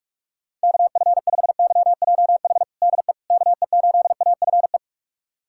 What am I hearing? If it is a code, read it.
Morse code, heard as a keyed tone, about 45 wpm, 700 Hz. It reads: KF5YJH DE KE8AFE
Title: USA Callsign DE Callsign - 6 Characters - 45wpm